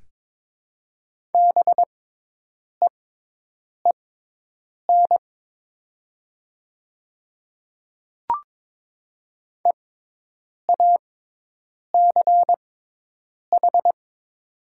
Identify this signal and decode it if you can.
Morse code, heard as a keyed tone, about 22 wpm, spaced as for 8 wpm, 700 Hz. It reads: BEEN EACH
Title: 6x Character Spacing - Top 100 Words - 22wpm